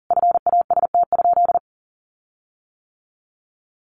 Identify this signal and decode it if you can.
Morse code, heard as a keyed tone, about 40 wpm, 700 Hz. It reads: FAST?
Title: Sentences from Top 1300 Words 40wpm